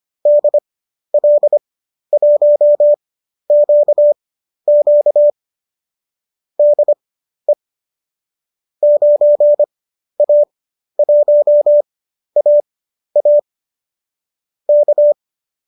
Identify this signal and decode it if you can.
Morse code, heard as a keyed tone, about 25 wpm, spaced as for 12 wpm, 590 Hz. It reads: DL1QQ DE 9A1AA K
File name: qso012 12